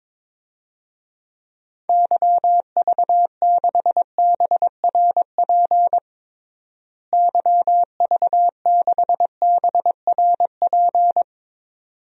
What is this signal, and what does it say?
Morse code, heard as a keyed tone, about 22 wpm, 700 Hz. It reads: YV6BRP YV6BRP
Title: DXCC-100 Call Signs - 6 Characters - Repeated 2x - 22wpm